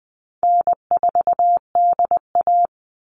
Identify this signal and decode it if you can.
Morse code, heard as a keyed tone, about 20 wpm, 700 Hz. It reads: N4DA
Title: NA SOTA Chaser Callsigns 20wpm